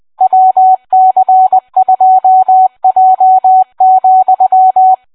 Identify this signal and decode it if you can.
Morse code, heard as a keyed tone, about 20 wpm, 750 Hz, with radio noise. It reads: WC2J,